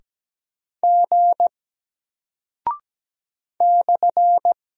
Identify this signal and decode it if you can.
Morse code, heard as a keyed tone, about 17 wpm, 700 Hz. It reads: G /
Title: Single Character (Letter:Number:Punctuation:Prosign) Alt 17wpm